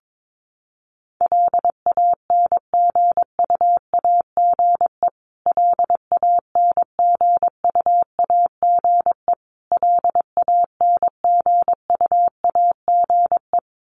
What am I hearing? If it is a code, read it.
Morse code, heard as a keyed tone, about 22 wpm, 700 Hz. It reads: LANGUAGE LANGUAGE LANGUAGE
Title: Top 500 Words Repeated 22wpm